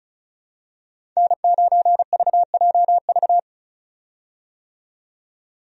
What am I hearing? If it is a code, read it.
Morse code, heard as a keyed tone, about 36 wpm, 700 Hz. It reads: N9VJV
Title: USA Call Signs - 5 Characters - 35wpm